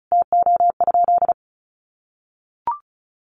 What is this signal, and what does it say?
Morse code, heard as a keyed tone, about 34 wpm, 700 Hz. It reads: TO?
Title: Sentences from Top 500 Words 35wpm